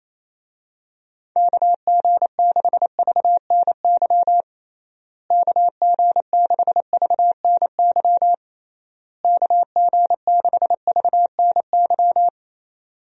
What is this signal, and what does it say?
Morse code, heard as a keyed tone, about 28 wpm, 700 Hz. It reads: KG6VNY KG6VNY KG6VNY
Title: USA Call Signs - 6 Characters - Repeated 3x - 28wpm